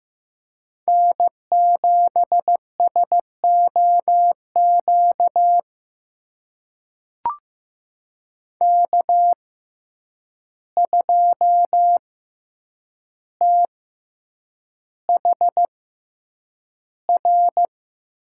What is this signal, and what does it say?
Morse code, heard as a keyed tone, about 15 wpm, spaced as for 8 wpm, 700 Hz. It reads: N7SOQ K2THR
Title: USA Call Signs - 5 Characters - 6x Character Spacing - 15wpm